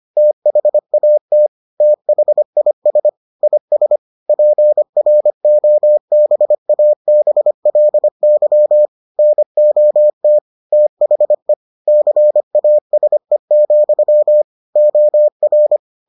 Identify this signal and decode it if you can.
Morse code, heard as a keyed tone, about 25 wpm, 600 Hz. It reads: THAT THIS IS PROBABLY NOT THE CASE, OR